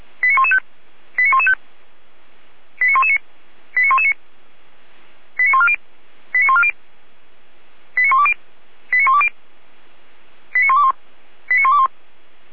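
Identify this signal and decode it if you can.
Five-tone selective call (ZVEI), heard as quick runs of five tones, tones of about 70 ms. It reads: 89115 89115 89119 89119 89140 89140 89120 89120 89121 89121
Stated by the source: Five-tone ZVEI selective call received by SDR, from the feuerwehr-orlamuende.org fire brigade audio archive.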